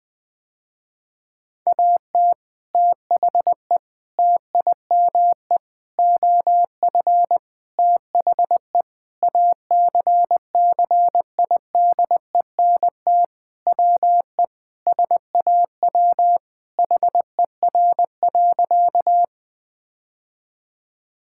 Morse code, heard as a keyed tone, about 21 wpm, 700 Hz. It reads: AT THE TIME OF THE ACCIDENT WE SAW HER.
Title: Sentences with 4-Word Phrases - Spoken Beginning - 20wpm